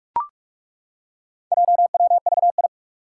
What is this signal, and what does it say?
Morse code, heard as a keyed tone, about 42 wpm, 700 Hz. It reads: JWVI
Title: Four Challenging Letter-Number Combinations 45wpm